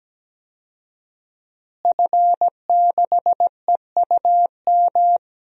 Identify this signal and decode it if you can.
Morse code, heard as a keyed tone, about 17 wpm, 700 Hz. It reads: F6EUM
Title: DXCC-100 Call Signs - 5 Characters - Repeated 2x - 17wpm